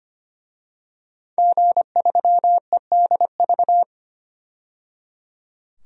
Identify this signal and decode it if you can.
Morse code, heard as a keyed tone, about 25 wpm, 700 Hz. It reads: G3EDV
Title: DXCC-100 Call Signs - 5 Characters - 25wpm